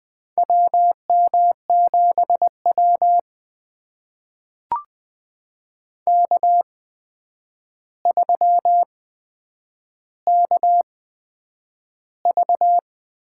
Morse code, heard as a keyed tone, about 20 wpm, spaced as for 9 wpm, 700 Hz. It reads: WM7W K3KV
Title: USA Call Signs - 4 Characters - 8x Character Spacing - 20wpm